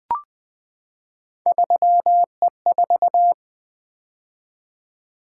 Morse code, heard as a keyed tone, about 20 wpm, 700 Hz. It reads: 3E4
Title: Three Letter-Number Combinations - 20wpm